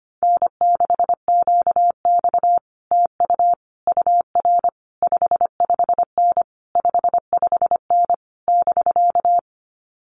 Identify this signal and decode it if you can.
Morse code, heard as a keyed tone, about 25 wpm, 700 Hz. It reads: N6QX TU UR 55N 55N <BK>
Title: Common QSO Phrases Encore 25wpm